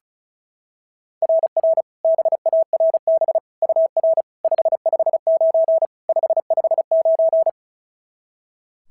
Morse code, heard as a keyed tone, about 36 wpm, 690 Hz. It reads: RR BARB UR 559 559
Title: SOTA QSO Practice 35wpm